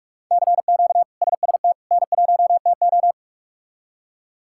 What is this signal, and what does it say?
Morse code, heard as a keyed tone, about 41 wpm, 700 Hz. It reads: CQ SST N1TO
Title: K1USN SST QSO Practice - 45wpm